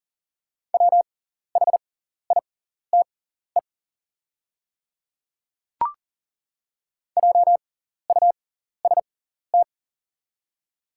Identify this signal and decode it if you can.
Morse code, heard as a keyed tone, about 40 wpm, spaced as for 14 wpm, 700 Hz. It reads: WHITE JUST
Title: 6x Character Spacing - Top 300 Words - 40wpm